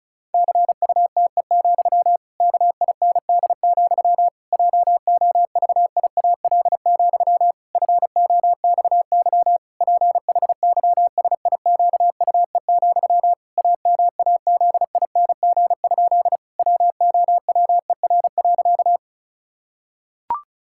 Morse code, heard as a keyed tone, about 34 wpm, 700 Hz. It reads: CUTE, KIND, JOVIAL, FOXY PHYSIQUE, AMAZING? WOWER.
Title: Pangram Challenge v2 - 35wpm